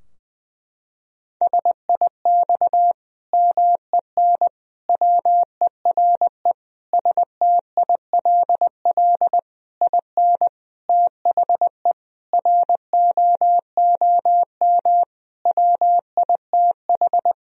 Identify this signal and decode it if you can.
Morse code, heard as a keyed tone, about 20 wpm, 700 Hz. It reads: SIX MEN WERE STILL IN THE ROOM WITH
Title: Sentences from Top 1000 Words - Review Part 2 - 20wpm